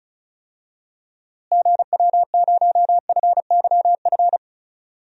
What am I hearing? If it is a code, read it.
Morse code, heard as a keyed tone, about 36 wpm, 700 Hz. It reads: GW0FYF
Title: DXCC-100 Call Signs - 6 Characters - Repeated 3x - 35wpm